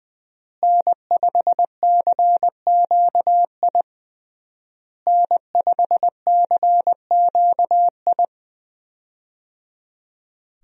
Morse code, heard as a keyed tone, about 20 wpm, 700 Hz. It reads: N5CQI N5CQI
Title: USA Call Signs - 5 Characters - Repeated 3x - 20wpm